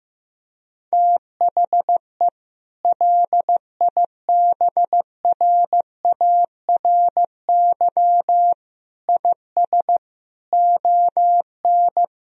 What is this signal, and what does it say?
Morse code, heard as a keyed tone, about 16 wpm, 700 Hz. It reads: THE LIBRARY IS ON
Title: Sentences from Top 1000 Words - Review Part 4 - 15wpm